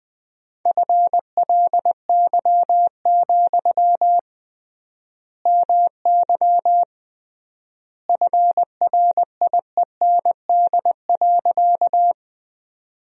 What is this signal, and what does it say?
Morse code, heard as a keyed tone, about 21 wpm, 700 Hz. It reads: FLY, MY FRIEND.
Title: 3x Word-Spacing - Sentences from Top 500 Words - 20wpm